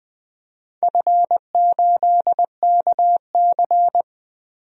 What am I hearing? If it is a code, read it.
Morse code, heard as a keyed tone, about 20 wpm, 700 Hz. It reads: F8KC
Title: DXCC-100 Call Signs - 4 Characters - Repeated 3x - 20wpm